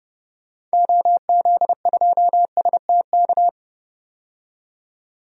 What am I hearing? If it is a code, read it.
Morse code, heard as a keyed tone, about 30 wpm, 700 Hz. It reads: OZ2STK
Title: DXCC-100 Call Signs - 6 Characters - 30wpm